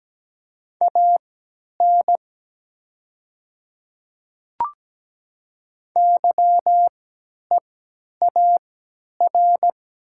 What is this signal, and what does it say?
Morse code, heard as a keyed tone, about 17 wpm, spaced as for 10 wpm, 690 Hz. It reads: AN YEAR
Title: 3x Character Spacing - Top 200 Words - 17wpm